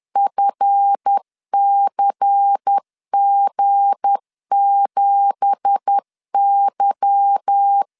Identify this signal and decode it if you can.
Morse code, heard as a keyed tone, about 10 wpm, 800 Hz. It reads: FCG7Y
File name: LEZIONI / 9 2867 10wpm 2733